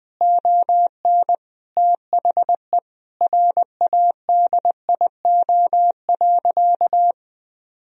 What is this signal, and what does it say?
Morse code, heard as a keyed tone, about 20 wpm, 700 Hz. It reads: ON THE RADIO.
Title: Sentences from Top 1000 Words - Review Part 3 - 20wpm